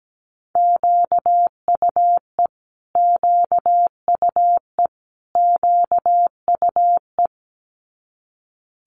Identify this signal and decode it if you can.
Morse code, heard as a keyed tone, about 17 wpm, 700 Hz. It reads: QUE QUE QUE
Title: Multi-letter Phonemes Repeated 3x 17wpm